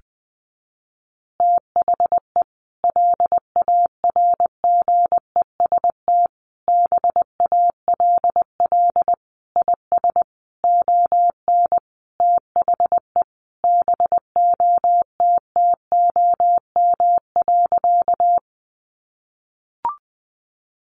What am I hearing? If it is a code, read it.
Morse code, heard as a keyed tone, about 20 wpm, 700 Hz. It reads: THE LARGEST BALL IS ON THE BOTTOM.
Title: Easy Phrases 20wpm